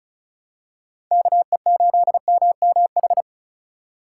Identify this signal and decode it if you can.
Morse code, heard as a keyed tone, about 36 wpm, 700 Hz. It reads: KE8MMH
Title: USA Call Signs - 6 Characters - 2x Character Spacing - 35wpm